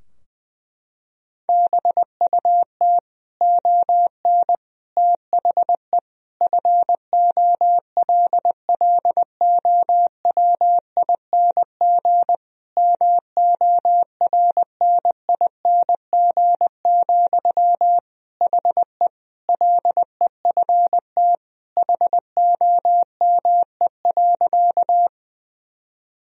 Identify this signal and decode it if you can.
Morse code, heard as a keyed tone, about 20 wpm, 700 Hz. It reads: BUT ON THE FOLLOWING MORNING, HE LEFT HOME.
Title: Sentences from Top 1000 Words - Review Part 4 - 20wpm